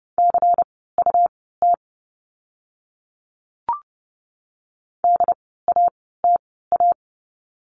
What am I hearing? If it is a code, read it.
Morse code, heard as a keyed tone, about 30 wpm, spaced as for 17 wpm, 700 Hz. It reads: CUT DATA
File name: Lesson 17f - Letter u - Words 30wpm